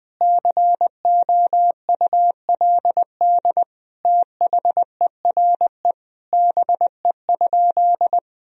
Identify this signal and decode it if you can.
Morse code, heard as a keyed tone, about 21 wpm, 700 Hz. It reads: COULD THERE BE?
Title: Sentences from Top 1000 Words - Review Part 3 - 20wpm